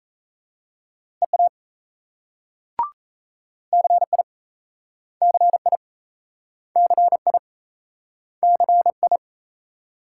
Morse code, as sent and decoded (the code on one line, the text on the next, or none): . .- / -.-. .. / -.-. .. / -.-. .. / -.-. ..
EA CI CI CI CI